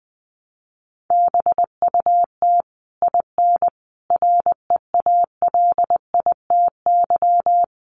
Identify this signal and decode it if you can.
Morse code, heard as a keyed tone, about 20 wpm, 700 Hz. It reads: BUT IN REALITY
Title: Sentences from Top 1200 Words 20wpm